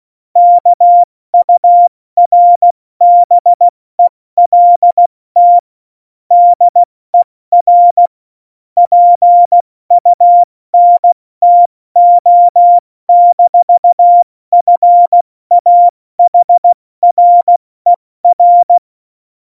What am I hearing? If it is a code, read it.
Morse code, heard as a keyed tone, about 16 wpm, 700 Hz. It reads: KURBELT DER PUNTO-FAHRER
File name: Morsekurs / Witze / Witz 04 August 2020, 70 F80